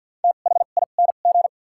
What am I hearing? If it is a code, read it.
Morse code, heard as a keyed tone, about 53 wpm, spaced as for 37 wpm, 700 Hz. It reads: THING
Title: 2x Character Spacing - Top 200 Words - 50wpm